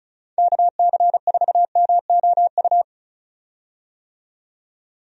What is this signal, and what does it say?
Morse code, heard as a keyed tone, about 36 wpm, 700 Hz. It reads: KC4MOU
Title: USA Call Signs - 6 Characters - 35wpm